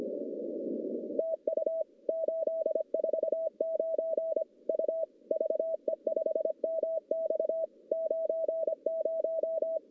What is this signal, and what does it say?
Morse code, heard as a keyed tone, about 25 wpm, 650 Hz. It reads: TU 849 U VE5MX 90
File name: contests / CW-SS-k5tr- -1100